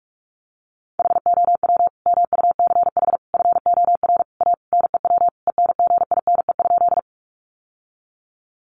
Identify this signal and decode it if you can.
Morse code, heard as a keyed tone, about 44 wpm, 700 Hz. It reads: HOW MUCH FOR A NEW ENGINE?
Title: Sentences from Top 1500 Words 45wpm